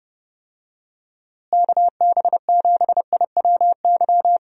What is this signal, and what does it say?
Morse code, heard as a keyed tone, about 32 wpm, 700 Hz. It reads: KB7IWY
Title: USA Call Signs - 6 Characters - Repeated 3x - 30wpm